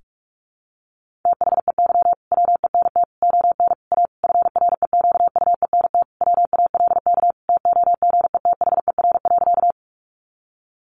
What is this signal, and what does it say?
Morse code, heard as a keyed tone, about 44 wpm, 700 Hz. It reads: THEY WENT ON A FREQUENT WALK TOGETHER.
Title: Sentences from Top 1400 Words 45wpm